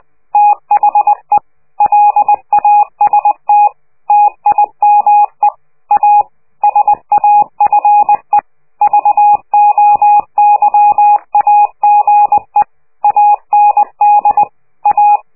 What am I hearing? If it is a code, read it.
Morse code, heard as a keyed tone, about 20 wpm, 830 Hz. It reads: THE LAST TIME A SAFE VOYAGE AND A
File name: worldcoastal / scheveningenradio / morsepch